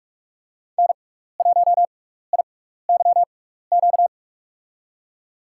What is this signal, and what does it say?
Morse code, heard as a keyed tone, about 42 wpm, spaced as for 15 wpm, 700 Hz. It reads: N1IYQ
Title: USA Call Signs - 6x Character Spacing - 45wpm